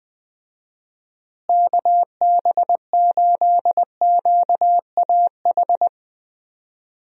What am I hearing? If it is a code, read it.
Morse code, heard as a keyed tone, about 20 wpm, 700 Hz. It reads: KB8QAH
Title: USA Call Signs - 6 Characters - Repeated 2x - 20wpm